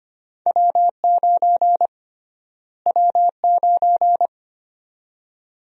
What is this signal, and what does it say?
Morse code, heard as a keyed tone, about 25 wpm, 700 Hz. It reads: W9 W9
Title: USA Call Sign Prefixes - Repeated 2x - 25wpm